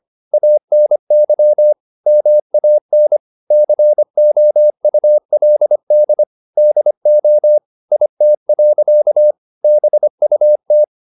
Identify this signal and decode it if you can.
Morse code, heard as a keyed tone, about 25 wpm, 600 Hz. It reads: ANY MAN COULD DO IT. BUT